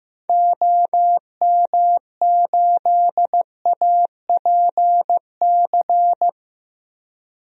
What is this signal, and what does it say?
Morse code, heard as a keyed tone, about 15 wpm, 700 Hz. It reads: OM8APC
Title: DXCC-100 Call Signs - 6 Characters - Repeated 3x - 15wpm